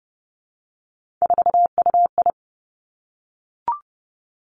Morse code, heard as a keyed tone, about 30 wpm, 700 Hz. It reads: 4UI
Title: Three Challenging Letter-Number Combinations 30wpm